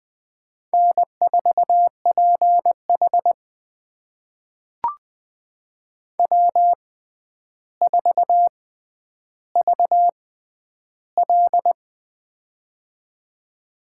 Morse code, heard as a keyed tone, about 21 wpm, spaced as for 10 wpm, 700 Hz. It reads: N4PH W4VL
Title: USA Call Signs - 4 Characters - 6x Character Spacing - 20wpm